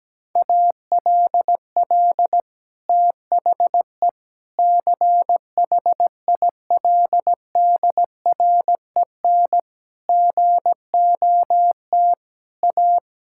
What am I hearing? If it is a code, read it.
Morse code, heard as a keyed tone, about 17 wpm, 700 Hz. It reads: ALL THE CHILDREN GOT A
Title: Sentences from Top 1000 Words - Review Part 6 - 17wpm